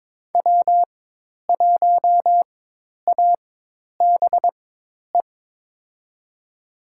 Morse code, heard as a keyed tone, about 22 wpm, spaced as for 10 wpm, 700 Hz. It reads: W1ABE
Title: USA Call Signs - 4x Character Spacing - 22wpm